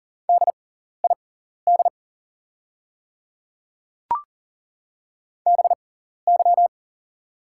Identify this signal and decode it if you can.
Morse code, heard as a keyed tone, about 40 wpm, spaced as for 14 wpm, 690 Hz. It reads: DID BY
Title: 6x Character Spacing - Top 200 Words - 40wpm